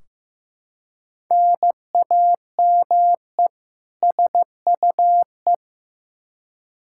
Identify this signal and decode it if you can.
Morse code, heard as a keyed tone, about 15 wpm, 700 Hz. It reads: NAME SUE
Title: Practice QSO - Exchange 1 - Beginning - 15wpm